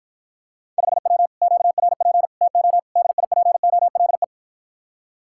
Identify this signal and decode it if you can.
Morse code, heard as a keyed tone, about 54 wpm, 700 Hz. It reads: 5W QRP TO DIPOLE